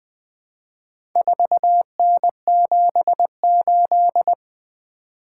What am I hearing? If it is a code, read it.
Morse code, heard as a keyed tone, about 21 wpm, 700 Hz. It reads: 4N78
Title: Four Letter-Number Combinations - 20wpm